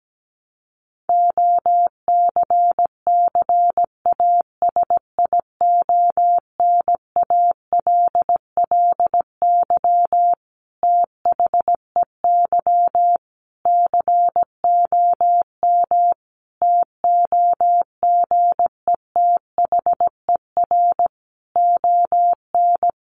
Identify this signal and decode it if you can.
Morse code, heard as a keyed tone, about 17 wpm, 700 Hz. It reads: OCCASIONALLY THEY COM TOGETHER ON
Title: Sentences from Top 1300 Words 17wpm